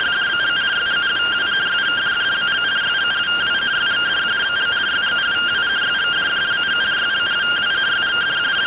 Two-tone teleprinter signal, about 170 Hz shift.